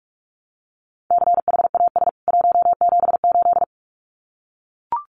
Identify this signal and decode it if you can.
Morse code, heard as a keyed tone, about 44 wpm, 700 Hz. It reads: CHAS 178
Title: CWT Exchange 45wpm